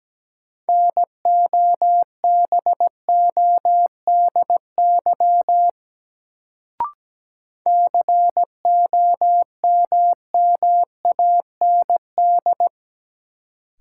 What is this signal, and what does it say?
Morse code, heard as a keyed tone, about 17 wpm, 700 Hz. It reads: NOBODY COMMAND
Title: Sentences from Top 1600 Words - New Word Drill - 17wpm